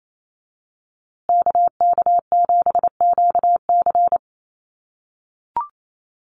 Morse code, heard as a keyed tone, about 28 wpm, 700 Hz. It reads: KK7QC
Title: USA Call Signs Repeated 2x 28wpm